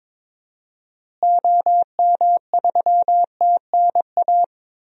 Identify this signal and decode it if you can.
Morse code, heard as a keyed tone, about 22 wpm, 700 Hz. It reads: OM3TNA
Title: DXCC-100 Call Signs - 6 Characters - Repeated 3x - 22wpm